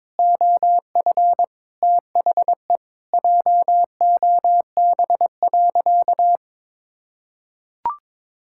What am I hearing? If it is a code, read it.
Morse code, heard as a keyed tone, about 21 wpm, 700 Hz. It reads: OF THE JOB.
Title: Sentences from Top 1000 Words - Review Part 5 - 22wpm